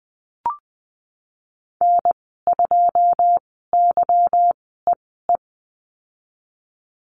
Morse code, heard as a keyed tone, about 20 wpm, spaced as for 15 wpm, 700 Hz. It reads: N2YEE
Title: 193 Lesson 31g - Letter v - Call Signs 20wpm